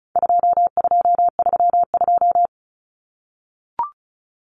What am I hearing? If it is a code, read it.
Morse code, heard as a keyed tone, about 34 wpm, 700 Hz. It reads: 2232
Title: Common Cut Numbers - 4-Digit 35wpm